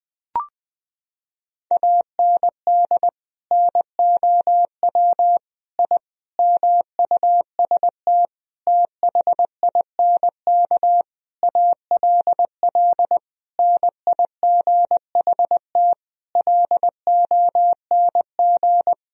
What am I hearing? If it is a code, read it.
Morse code, heard as a keyed tone, about 20 wpm, 700 Hz. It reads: AND NOW I MUST THINK ALL NIGHT LONG
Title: Sentences from Top 1000 Words - Review Part 3 - 20wpm